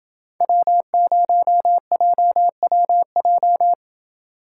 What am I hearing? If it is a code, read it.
Morse code, heard as a keyed tone, about 29 wpm, 700 Hz. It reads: W0JWJ